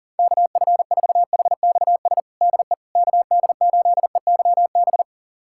Morse code, heard as a keyed tone, about 37 wpm, 700 Hz. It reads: KF4HXS DE KD8EYB
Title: USA Callsign DE Callsign - 6 Characters - 40wpm